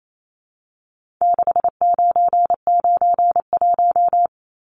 Morse code, heard as a keyed tone, about 28 wpm, 700 Hz. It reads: 6991
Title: Numbers - 4-Digit 28wpm